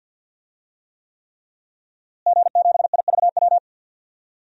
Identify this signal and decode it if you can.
Morse code, heard as a keyed tone, about 50 wpm, 700 Hz. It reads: G7IVW
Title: DXCC-100 Call Signs - 5 Characters - Repeated 3x - 50wpm